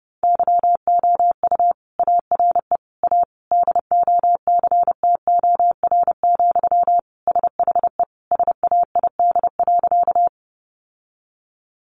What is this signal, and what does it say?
Morse code, heard as a keyed tone, about 30 wpm, 700 Hz. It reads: YOU ARE A DOCTOR, SHE SAID.
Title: Sentences from Top 1500 Words 30wpm